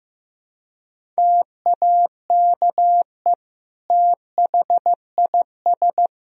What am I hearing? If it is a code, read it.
Morse code, heard as a keyed tone, about 15 wpm, 700 Hz. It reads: TAKE THIS
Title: Sentences from Top 200 Words and Abbreviations 15wpm